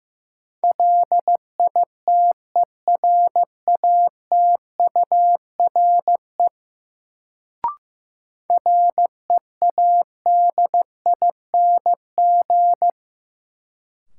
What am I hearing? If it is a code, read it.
Morse code, heard as a keyed tone, about 15 wpm, 700 Hz. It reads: LITERATURE READING
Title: Sentences from Top 600 Words - New Word Drill - 15wpm